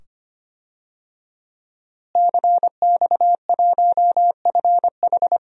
Morse code, heard as a keyed tone, about 25 wpm, 700 Hz. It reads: CX1FH
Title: DXCC-100 Call Signs - 5 Characters - Repeated 2x - 25wpm